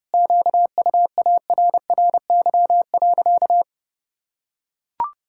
Morse code, heard as a keyed tone, about 30 wpm, 700 Hz. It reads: QUARRY.
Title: Pangram Challenge v2 - 30wpm